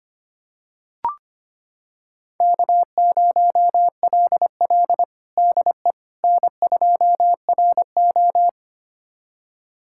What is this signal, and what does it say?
Morse code, heard as a keyed tone, about 25 wpm, 700 Hz. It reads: K0LL DE N2RO
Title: USA Callsign DE Callsign - 4 Characters - 25wpm